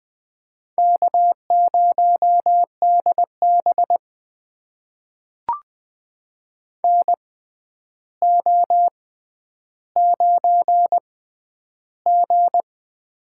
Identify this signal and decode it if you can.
Morse code, heard as a keyed tone, about 21 wpm, spaced as for 10 wpm, 700 Hz. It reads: K0DB NO9G
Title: USA Call Signs - 4 Characters - 6x Character Spacing - 20wpm